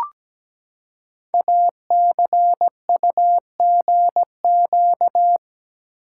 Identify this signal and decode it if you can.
Morse code, heard as a keyed tone, about 17 wpm, 700 Hz. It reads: ACUGQ
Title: Code Groups - 5 Letters - 17wpm